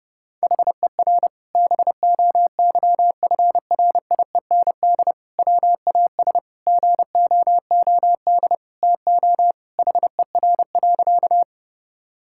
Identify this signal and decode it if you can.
Morse code, heard as a keyed tone, about 30 wpm, 700 Hz. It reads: HER BOYFRIEND WAS GOOD TO HER.
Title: Sentences from Top 1000 Words - Review Part 5 - 30wpm